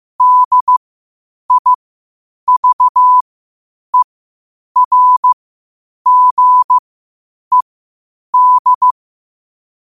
Morse code, heard as a keyed tone, about 15 wpm, spaced as for 8 wpm, 1000 Hz. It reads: DIVERGED
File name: qotdfiles / 8wpm